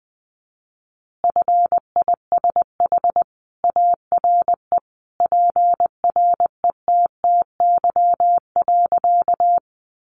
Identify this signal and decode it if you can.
Morse code, heard as a keyed tone, about 20 wpm, 700 Hz. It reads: FISH ARE PRETTY.
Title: Easy Phrases 20wpm